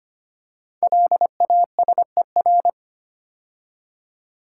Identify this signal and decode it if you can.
Morse code, heard as a keyed tone, about 25 wpm, 700 Hz. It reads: LASER
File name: 5 Letter Words - First 3 Letters Given - 25wpm